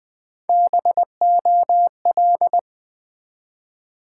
Morse code, heard as a keyed tone, about 20 wpm, 700 Hz. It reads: BOL
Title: Build A Word - 1 Letter At A Time - 4-Letter Words - 20wpm